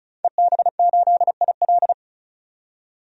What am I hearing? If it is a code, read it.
Morse code, heard as a keyed tone, about 34 wpm, 700 Hz. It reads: EB8IL
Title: DXCC-100 Call Signs - 5 Characters - 35wpm